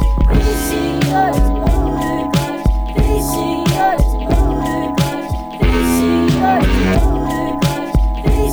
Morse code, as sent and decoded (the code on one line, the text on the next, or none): none